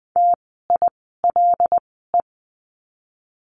Morse code, heard as a keyed tone, about 20 wpm, spaced as for 14 wpm, 700 Hz. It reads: TILE
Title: 075 Lesson 13f - Letter L - Words 20wpm